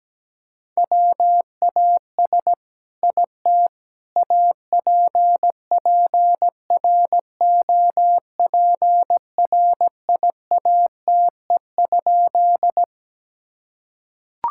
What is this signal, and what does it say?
Morse code, heard as a keyed tone, about 17 wpm, 700 Hz. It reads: WAS IT APPROPRIATE?
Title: Sentences from Top 1000 Words - Review Part 5 - 17wpm